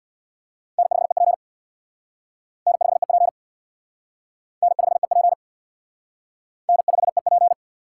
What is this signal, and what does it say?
Morse code, heard as a keyed tone, about 66 wpm, 700 Hz. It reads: N5EP N5EP N5EP N5EP